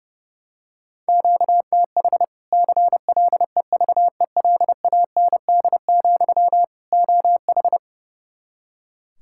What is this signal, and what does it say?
Morse code, heard as a keyed tone, about 32 wpm, 700 Hz. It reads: QTH CLEVELAND, OH
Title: Practice QSO - Exchange 1 - Beginning - 30wpm